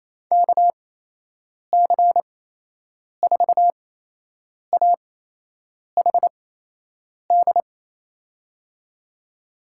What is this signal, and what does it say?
Morse code, heard as a keyed tone, about 28 wpm, spaced as for 8 wpm, 700 Hz. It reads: KC4AHD
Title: USA Call Signs - 6 Characters - 8x Character Spacing - 28wpm